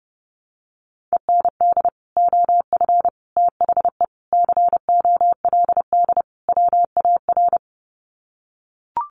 Morse code, heard as a keyed tone, about 30 wpm, 700 Hz. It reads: END OF THE COLD WAR
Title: Sets of 5 Words Encore 30wpm